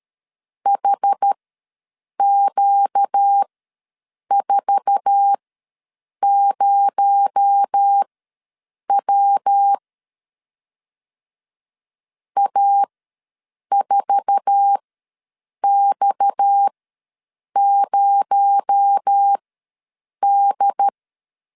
Morse code, heard as a keyed tone, about 13 wpm, spaced as for 7 wpm, 800 Hz. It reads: HQ40W A4X0D
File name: LEZIONI / 8 YJ45 / 8 YJ4 ws 12wpm 3249